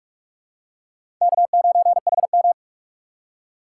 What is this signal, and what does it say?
Morse code, heard as a keyed tone, about 44 wpm, 700 Hz. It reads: K9HM
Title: USA Call Signs - 4 Characters - 4x Character Spacing - 45wpm